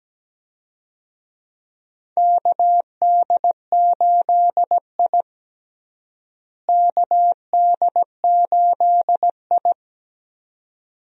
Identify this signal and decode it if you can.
Morse code, heard as a keyed tone, about 17 wpm, 700 Hz. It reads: KD8I KD8I
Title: USA Call Signs - 4 Characters - Repeated 2x - 17wpm